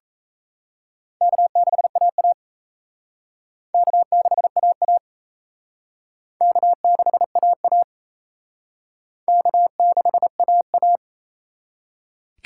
Morse code, heard as a keyed tone, about 36 wpm, 700 Hz. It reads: K6AA K6AA K6AA K6AA